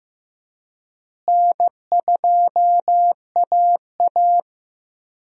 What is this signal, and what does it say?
Morse code, heard as a keyed tone, about 15 wpm, 700 Hz. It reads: N2AA
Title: USA Call Signs - 4 Characters - Repeated 3x - 15wpm